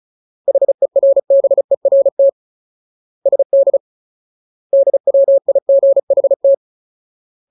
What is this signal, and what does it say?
Morse code, heard as a keyed tone, about 34 wpm, 550 Hz. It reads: HERBERT SD DWIGHT
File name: Ultimate Word List / 35WPM Ultimate